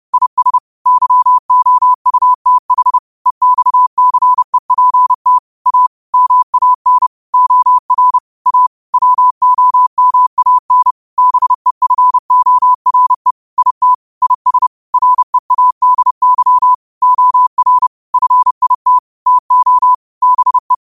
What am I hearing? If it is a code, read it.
Morse code, heard as a keyed tone, about 30 wpm, 1000 Hz. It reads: IS YOUTH EXCEPT A MAN OR A WOMAN BEFORE IT IS READY OR FIT TO BE
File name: qotdfiles / 30wpm